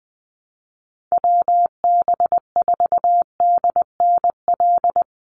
Morse code, heard as a keyed tone, about 20 wpm, 700 Hz. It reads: WB4DNL
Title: Call Signs - 20wpm